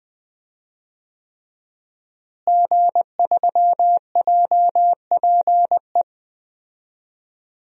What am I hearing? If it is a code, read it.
Morse code, heard as a keyed tone, about 20 wpm, 700 Hz. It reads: G3JPE
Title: DXCC-100 Call Signs - 5 Characters - Repeated 3x - 20wpm